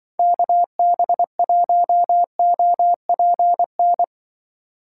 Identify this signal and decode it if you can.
Morse code, heard as a keyed tone, about 25 wpm, 700 Hz. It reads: KB1OPN
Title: Speed Racing - USA 2x3 Call Signs 20wpm